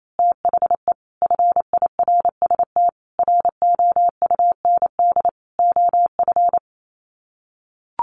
Morse code, heard as a keyed tone, about 28 wpm, 700 Hz. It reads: THE FIRST ROUND OF
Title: Sets of 5 Words Encore 28wpm